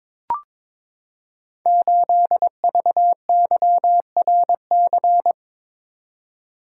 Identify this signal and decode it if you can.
Morse code, heard as a keyed tone, about 22 wpm, 700 Hz. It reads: 8VYRC
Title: Code Groups - 5 Letter:Numbers - 22wpm